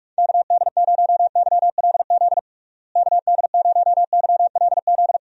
Morse code, heard as a keyed tone, about 41 wpm, 700 Hz. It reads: KD0YLZ KD0YLZ
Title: USA Call Signs - 6 Characters - Repeated 2x - 45wpm